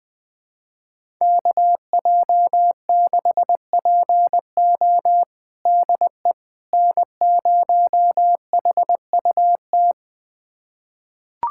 Morse code, heard as a keyed tone, about 21 wpm, 700 Hz. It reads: KJ6PO DE N0HUT
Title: USA Callsign DE Callsign - 5 Characters - 20wpm